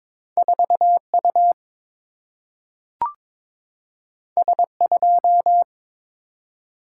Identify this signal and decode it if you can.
Morse code, heard as a keyed tone, about 22 wpm, 700 Hz. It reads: 4 U S 2
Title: Single-Letter Single-Number 22wpm